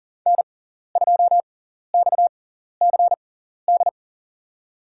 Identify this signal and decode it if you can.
Morse code, heard as a keyed tone, about 40 wpm, spaced as for 14 wpm, 700 Hz. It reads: N2XCD
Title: USA Call Signs - 6x Character Spacing - 40wpm